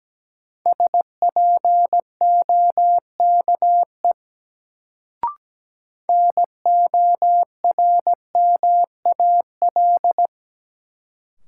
Morse code, heard as a keyed tone, about 17 wpm, 700 Hz. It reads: SPOKE NORMAL
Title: Sentences from Top 1200 Words - New Word Drill - 17wpm